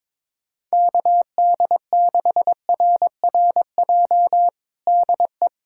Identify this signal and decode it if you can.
Morse code, heard as a keyed tone, about 22 wpm, 700 Hz. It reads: KD6RRJ DE
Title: USA Callsign DE Callsign - 6 Characters - 22wpm